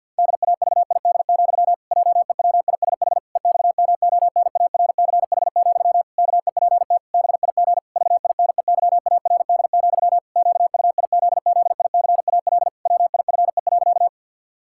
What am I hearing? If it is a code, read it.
Morse code, heard as a keyed tone, about 52 wpm, 700 Hz. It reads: DAVID, JEWISH EXMONARCH, KEPT BIG VINEYARD, QUIZZICAL WIFE.
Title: Pangram Challenge v2 - 50wpm